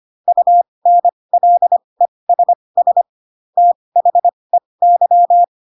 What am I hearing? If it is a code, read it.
Morse code, heard as a keyed tone, about 25 wpm, spaced as for 20 wpm, 700 Hz. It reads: UNLESS THEY
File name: Pinocchio / Pinocchio ch8 15 25 ik1hge